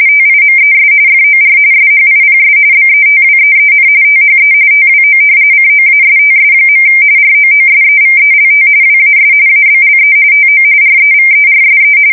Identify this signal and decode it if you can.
RTTY, about 45.45 baud. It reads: TIMES FOR SHIPS BETWEEN THE TWO COASTS. ROOSEVELT WAS ABLE TO REVERSE
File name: fsk170 / usnavy3